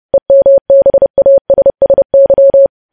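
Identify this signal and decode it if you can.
Morse code, heard as a keyed tone, about 30 wpm, 550 Hz. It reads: EMBASSY
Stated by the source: Morse news headlines sent at 30 wpm, opening with vvv.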